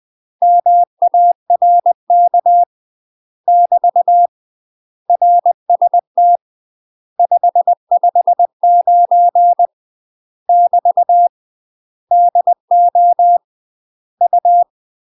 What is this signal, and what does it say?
Morse code, heard as a keyed tone, about 20 wpm, 700 Hz. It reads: MARK <BT> RST 559 <BT> DO U
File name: UK MASTER QSOs TRAINING / 20WPM / UK MASTER QSOs 0049